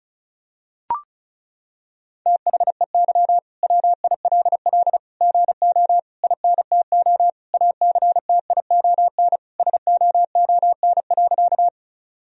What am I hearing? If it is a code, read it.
Morse code, heard as a keyed tone, about 36 wpm, 700 Hz. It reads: THEY WILL GO INTO ACTION SOON.
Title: Sentences from Top 1000 Words - Review Part 4 - 35wpm